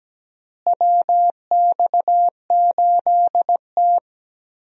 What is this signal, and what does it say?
Morse code, heard as a keyed tone, about 17 wpm, 700 Hz. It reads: WX8T
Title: USA Call Signs - 4 Characters - Repeated 3x - 17wpm